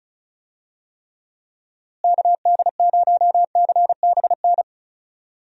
Morse code, heard as a keyed tone, about 36 wpm, 700 Hz. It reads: KD0CBN
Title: USA Call Signs - 6 Characters - Repeated 3x - 35wpm